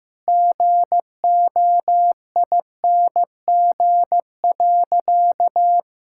Morse code, heard as a keyed tone, about 15 wpm, 700 Hz. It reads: GOING.
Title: Sentences from Top 1000 Words - Review Part 5 - 15wpm